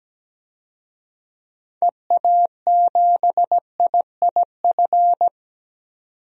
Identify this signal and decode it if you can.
Morse code, heard as a keyed tone, about 17 wpm, 700 Hz. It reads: EA7IIF
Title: DXCC-100 Call Signs - 6 Characters - 17wpm